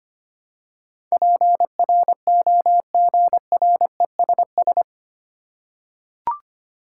Morse code, heard as a keyed tone, about 25 wpm, 700 Hz. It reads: PROGRESS
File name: Eight Letter Words - 25wpm